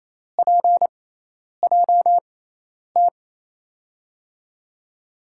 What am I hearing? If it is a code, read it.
Morse code, heard as a keyed tone, about 28 wpm, spaced as for 10 wpm, 700 Hz. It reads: PJT
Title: USA Call Signs - 6x Character Spacing - 28wpm